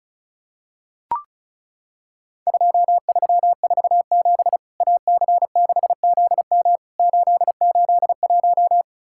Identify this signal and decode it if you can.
Morse code, heard as a keyed tone, about 36 wpm, 700 Hz. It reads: 2347 AC6ZM 881
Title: Worked All Europe DX Contest - QTC Exchange - 35wpm